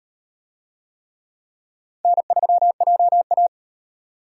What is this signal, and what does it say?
Morse code, heard as a keyed tone, about 35 wpm, 700 Hz. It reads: N3JA